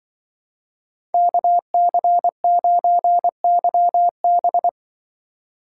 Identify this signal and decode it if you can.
Morse code, heard as a keyed tone, about 24 wpm, 700 Hz. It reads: KC9YB
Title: Speed Racing - USA 2x2 Call Signs 20wpm